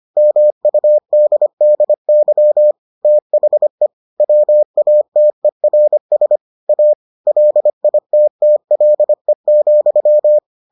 Morse code, heard as a keyed tone, about 25 wpm, 600 Hz. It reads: MUDDY THE WATERS A LITTLE,